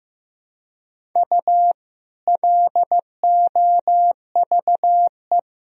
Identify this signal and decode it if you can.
Morse code, heard as a keyed tone, about 16 wpm, 700 Hz. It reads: U LOVE
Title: Sentences from Top 400 Words and Abbreviations 15wpm